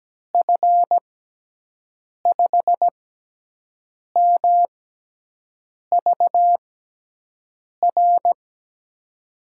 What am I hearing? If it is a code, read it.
Morse code, heard as a keyed tone, about 17 wpm, spaced as for 6 wpm, 700 Hz. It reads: F5MVR
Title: USA Call Signs - 6 Characters - 6x Character Spacing - 17wpm